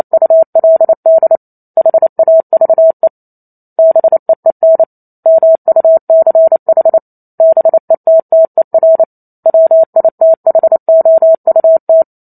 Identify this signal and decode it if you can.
Morse code, heard as a keyed tone, about 28 wpm, 650 Hz. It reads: ULD HAVE BEEN MUCH BETTER WITHOUT
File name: Chapter 2 - 28 WPM